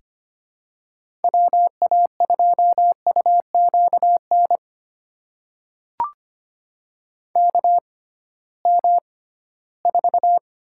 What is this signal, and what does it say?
Morse code, heard as a keyed tone, about 25 wpm, spaced as for 16 wpm, 700 Hz. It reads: WA2UQN KM4
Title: USA Call Signs - 6 Characters - 6x Character Spacing - 25wpm